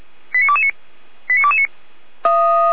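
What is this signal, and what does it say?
Five-tone selective call (ZVEI), heard as quick runs of five tones, tones of about 70 ms. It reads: 89229 89229
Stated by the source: ZVEI five-tone selective call received by SDR.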